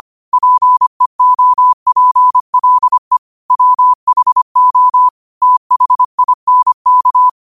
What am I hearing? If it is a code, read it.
Morse code, heard as a keyed tone, about 25 wpm, 1000 Hz. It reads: PEOPLE WHO THINK
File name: 25wpm